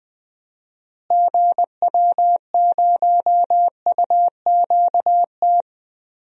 Morse code, heard as a keyed tone, about 21 wpm, 700 Hz. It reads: GW0UQT
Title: DXCC-100 Call Signs - 6 Characters - Repeated 2x - 20wpm